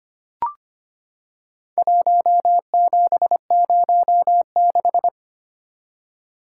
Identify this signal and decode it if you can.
Morse code, heard as a keyed tone, about 25 wpm, 700 Hz. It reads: 1706
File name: practice / 25wpm / Four Letter-Number Combinations - 25wpm